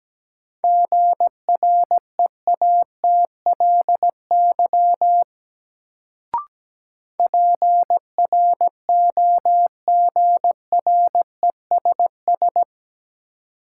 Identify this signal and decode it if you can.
Morse code, heard as a keyed tone, about 17 wpm, 700 Hz. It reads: GREATLY PROGRESS
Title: Sentences from Top 1400 Words - New Word Drill Alt - 17wpm